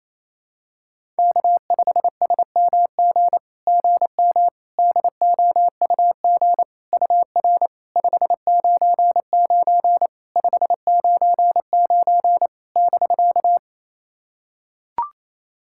Morse code, heard as a keyed tone, about 28 wpm, 700 Hz. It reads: K5SMG GM DOUG UR 599 599 <BK>
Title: SOTA QSO Practice 28wpm